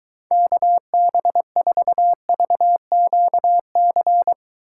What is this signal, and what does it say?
Morse code, heard as a keyed tone, about 23 wpm, 700 Hz. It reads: KB4VQC